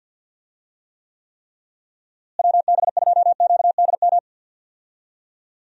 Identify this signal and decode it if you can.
Morse code, heard as a keyed tone, about 50 wpm, 700 Hz. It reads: WB2QDM
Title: USA Call Signs - 6 Characters - Repeated 3x - 50wpm